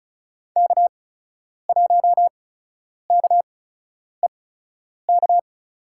Morse code, heard as a keyed tone, about 36 wpm, spaced as for 10 wpm, 700 Hz. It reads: K1KEK
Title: USA Call Signs - 5 Characters - 8x Character Spacing - 35wpm